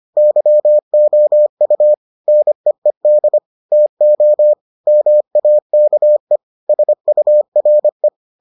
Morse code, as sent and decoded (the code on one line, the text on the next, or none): -.-- --- ..- / -. . -.. / - --- / -- .- -.- . / ... ..- .-. .
YOU NEED TO MAKE SURE